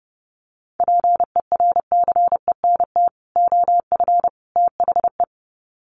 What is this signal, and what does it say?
Morse code, heard as a keyed tone, about 30 wpm, 700 Hz. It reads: PERCENT OF THE
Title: Sets of 3 Words 30wpm